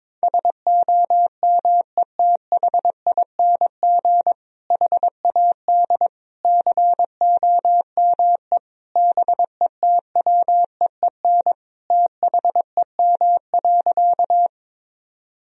Morse code, heard as a keyed tone, about 22 wpm, 700 Hz. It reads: SOMETHING HAD COME BETWEEN THEM.
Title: Sentences from Top 1000 Words - Review Part 1 - 22wpm